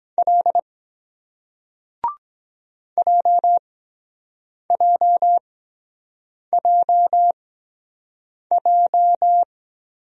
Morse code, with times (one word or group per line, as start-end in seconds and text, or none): 0.2-0.6 L
3.0-9.4 JJJJ